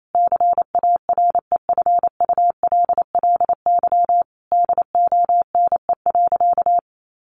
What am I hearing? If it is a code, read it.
Morse code, heard as a keyed tone, about 28 wpm, 700 Hz. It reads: CAREFULLY DONE.
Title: Sentences from Top 1300 Words 28wpm